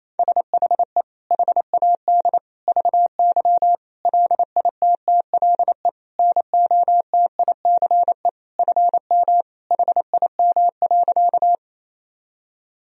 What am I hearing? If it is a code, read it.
Morse code, heard as a keyed tone, about 28 wpm, 700 Hz. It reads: SHE HAD VY LITTLE NOTICE FM HIM.
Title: Sentences from Top 400 Words and Abbreviations 28wpm